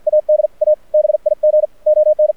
Morse code, heard as a keyed tone, about 48 wpm, 610 Hz. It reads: AD A BIG ON
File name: archived / CRSnet